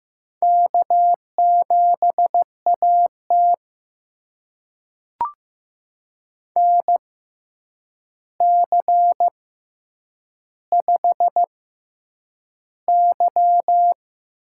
Morse code, heard as a keyed tone, about 15 wpm, spaced as for 8 wpm, 700 Hz. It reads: K7AT NC5Y
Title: USA Call Signs - 4 Characters - 6x Character Spacing - 15wpm